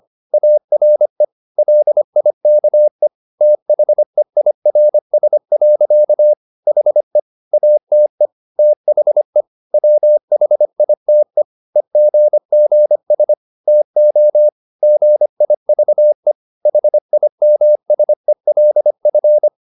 Morse code, heard as a keyed tone, about 25 wpm, 600 Hz. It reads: ARE LIKE THEIRS. HE ATE THE WHITE EGGS TO GIVE HIMSELF
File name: the old man and the sea part 1 25wpm